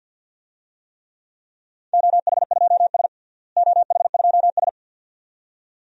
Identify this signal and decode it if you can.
Morse code, heard as a keyed tone, about 50 wpm, 700 Hz. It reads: OH2S OH2S
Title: DXCC-100 Call Signs - 4 Characters - Repeated 2x - 50wpm